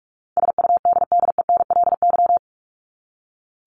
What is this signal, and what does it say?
Morse code, heard as a keyed tone, about 44 wpm, 700 Hz. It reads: SUDDENLY
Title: Top 1501-1750 Words 45wpm